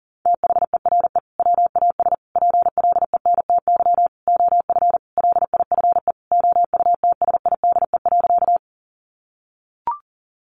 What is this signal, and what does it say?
Morse code, heard as a keyed tone, about 40 wpm, 700 Hz. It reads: THERE WAS PLENTY OF LIFE OUTSIDE.
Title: Sentences from Top 1300 Words 40wpm